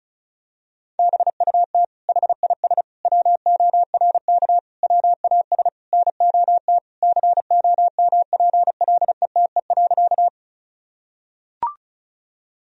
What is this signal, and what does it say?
Morse code, heard as a keyed tone, about 36 wpm, 700 Hz. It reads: BUT HIS WORK WAS NOT COMPLETE.
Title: Sentences from Top 1000 Words - Review Part 2 - 35wpm